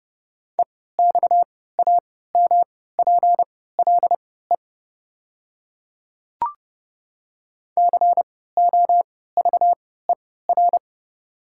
Morse code, heard as a keyed tone, about 30 wpm, spaced as for 17 wpm, 700 Hz. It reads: EXAMPLE COVER
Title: 3x Character Spacing - Top 300 Words - 30wpm